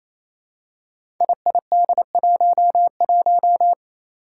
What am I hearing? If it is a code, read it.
Morse code, heard as a keyed tone, about 28 wpm, 700 Hz. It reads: IID11
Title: Five Challenging Letter-Number Combinations 28wpm